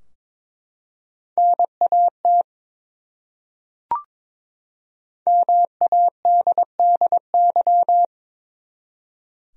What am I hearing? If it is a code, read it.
Morse code, heard as a keyed tone, about 23 wpm, 700 Hz. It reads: NAT MADDY
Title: USA Common Names - 22wpm